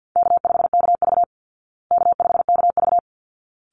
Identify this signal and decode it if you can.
Morse code, heard as a keyed tone, about 50 wpm, 700 Hz. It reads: K5KV K5KV
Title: Call Signs Repeated 3x - 50wpm